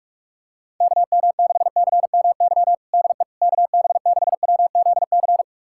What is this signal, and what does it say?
Morse code, heard as a keyed tone, about 41 wpm, 700 Hz. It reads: KM6CMY DE KB6WZC
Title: USA Callsign DE Callsign - 6 Characters - 45wpm